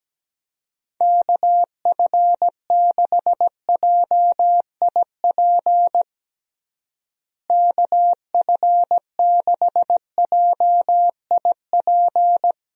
Morse code, heard as a keyed tone, about 17 wpm, 700 Hz. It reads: KF6JIP KF6JIP
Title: USA Call Signs - 6 Characters - Repeated 3x - 17wpm